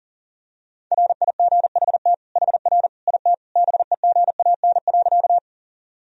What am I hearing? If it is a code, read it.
Morse code, heard as a keyed tone, about 40 wpm, 700 Hz. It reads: RIGHT HR IT BEGAN.
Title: Sentences from Top 300 Words and Abbreviations 40wpm